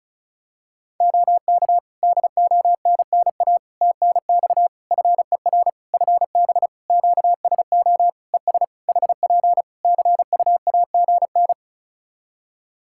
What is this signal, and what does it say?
Morse code, heard as a keyed tone, about 34 wpm, 700 Hz. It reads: OK DONNA TNX FER FB QSO ES HP CUAGN
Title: Practice QSO - Exchange 4 - Ending - 35wpm